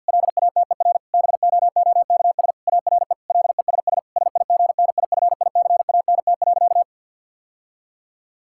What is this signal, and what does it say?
Morse code, heard as a keyed tone, about 47 wpm, 700 Hz. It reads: LATER BOOKS ARE LESS SIGNIFICANT.
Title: Sentences from Top 1000 Words - Review Part 6 - 50wpm